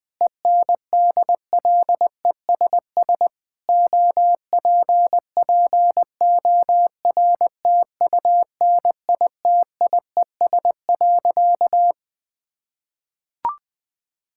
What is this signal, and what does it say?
Morse code, heard as a keyed tone, about 20 wpm, 700 Hz. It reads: ENDLESS OPPORTUNITIES.
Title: Difficult Sentences with Hints - 20wpm